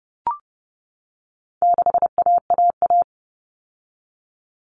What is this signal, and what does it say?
Morse code, heard as a keyed tone, about 30 wpm, 700 Hz. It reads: 6AAA
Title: Common Cut Numbers - 4-Digit 30wpm